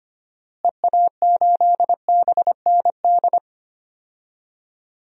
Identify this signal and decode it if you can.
Morse code, heard as a keyed tone, about 25 wpm, 700 Hz. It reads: EA8BND
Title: DXCC-100 Call Signs - 6 Characters - Repeated 3x - 25wpm